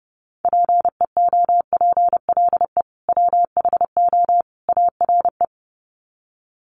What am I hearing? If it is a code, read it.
Morse code, heard as a keyed tone, about 30 wpm, 700 Hz. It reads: PEOPLE WHO ARE
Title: Sets of 3 Words 30wpm